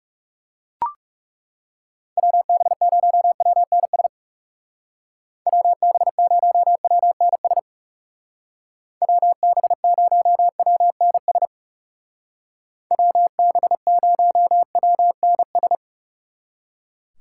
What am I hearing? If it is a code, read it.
Morse code, heard as a keyed tone, about 36 wpm, 700 Hz. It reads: WB0WNS WB0WNS WB0WNS WB0WNS
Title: Speed Racing - USA 2x3 Call Signs 30wpm